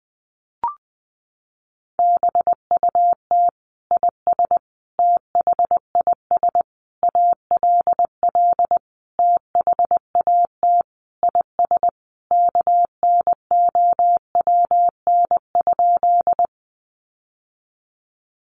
Morse code, heard as a keyed tone, about 20 wpm, 700 Hz. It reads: BUT IS THIS ALL THAT IS KNOWN?
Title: Sentences from Top 600 Words 20wpm